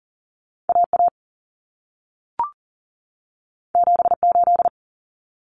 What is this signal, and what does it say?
Morse code, heard as a keyed tone, about 40 wpm, 700 Hz. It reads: AA 78
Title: Common Cut Numbers - 2-Digit 40wpm